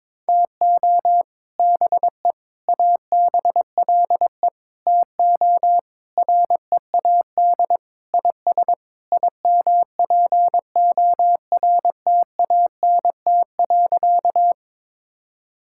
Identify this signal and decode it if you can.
Morse code, heard as a keyed tone, about 22 wpm, 700 Hz. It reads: TO BE ABLE TO READ IS IMPORTANT.
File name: practice / 22wpm / Sentences with 4-Word Phrases - 22wpm